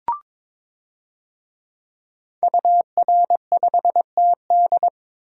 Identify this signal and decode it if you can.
Morse code, heard as a keyed tone, about 22 wpm, 700 Hz. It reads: UR5TD
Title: DXCC-100 Call Signs - 5 Characters - 22wpm